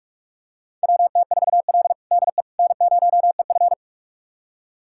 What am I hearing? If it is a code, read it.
Morse code, heard as a keyed tone, about 46 wpm, 700 Hz. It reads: WT4L DE N0EF
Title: USA Callsign DE Callsign - 4 Characters - 45wpm